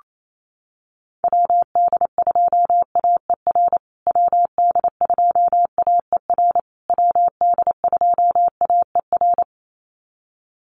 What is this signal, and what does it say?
Morse code, heard as a keyed tone, about 28 wpm, 700 Hz. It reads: WD2AER WD2AER WD2AER
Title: USA Call Signs Repeated 3x 28wpm